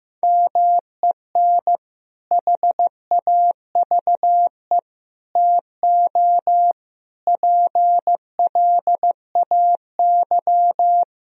Morse code, heard as a keyed tone, about 16 wpm, 700 Hz. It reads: MEN HAVE TO PLAY
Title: Sentences from Top 1000 Words - Review Part 3 - 15wpm